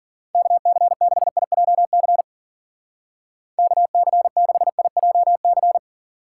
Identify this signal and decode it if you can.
Morse code, heard as a keyed tone, about 49 wpm, 700 Hz. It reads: KC6IJC KC6IJC
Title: Speed Racing - USA Call Signs 40wpm